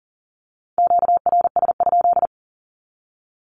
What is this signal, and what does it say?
Morse code, heard as a keyed tone, about 40 wpm, 700 Hz. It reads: QRS?
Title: Q-Codes 40wpm